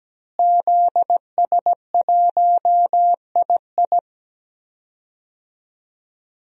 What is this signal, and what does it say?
Morse code, heard as a keyed tone, about 17 wpm, 700 Hz. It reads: ZS1II
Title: DXCC-100 Call Signs - 5 Characters - Repeated 3x - 17wpm